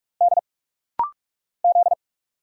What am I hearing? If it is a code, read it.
Morse code, heard as a keyed tone, about 46 wpm, 700 Hz. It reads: D Z
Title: Single Letter-Number Alt - Rapid-Fire 45wpm